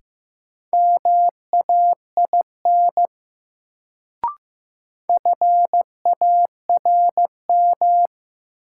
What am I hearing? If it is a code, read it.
Morse code, heard as a keyed tone, about 15 wpm, 700 Hz. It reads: MAIN FARM
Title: Sentences from Top 1000 Words - Review Part 2 - New Word Drill - 15wpm